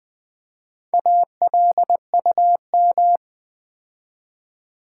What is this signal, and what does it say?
Morse code, heard as a keyed tone, about 20 wpm, 700 Hz. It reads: ALUM
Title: Build A Word - 4 Letters At A Time - 8-Letter Words - 20wpm